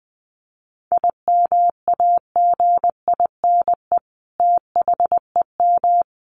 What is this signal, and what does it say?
Morse code, heard as a keyed tone, about 19 wpm, 700 Hz. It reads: IMAGINE THEM
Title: Sentences from Top 1400 Words 20wpm